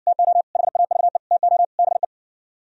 Morse code, heard as a keyed tone, about 60 wpm, 700 Hz. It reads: TO HAVE TO BE